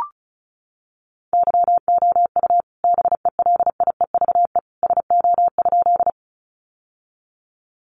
Morse code, heard as a keyed tone, about 34 wpm, 700 Hz. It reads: YOU BELIEVE SO?
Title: Sentences from Top 1000 Words 35wpm